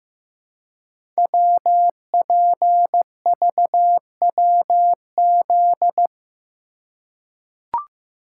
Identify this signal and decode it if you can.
Morse code, heard as a keyed tone, about 15 wpm, 700 Hz. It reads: WPVWZ
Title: Code Groups - 5 Letters - 15wpm